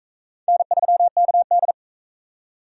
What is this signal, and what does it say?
Morse code, heard as a keyed tone, about 38 wpm, 700 Hz. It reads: N3KD